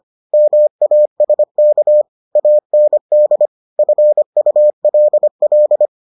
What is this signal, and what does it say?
Morse code, heard as a keyed tone, about 25 wpm, 600 Hz. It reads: MASK AND FULL